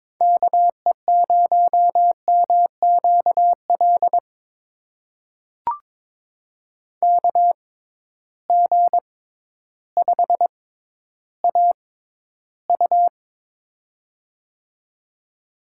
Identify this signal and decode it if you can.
Morse code, heard as a keyed tone, about 22 wpm, spaced as for 12 wpm, 700 Hz. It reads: KE0MQL KG5AU
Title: USA Call Signs - 6x Character Spacing - 22wpm